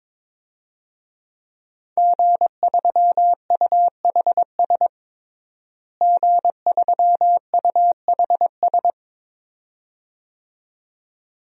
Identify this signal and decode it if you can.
Morse code, heard as a keyed tone, about 22 wpm, 700 Hz. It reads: G3UHS G3UHS
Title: DXCC-100 Call Signs - 5 Characters - Repeated 2x - 22wpm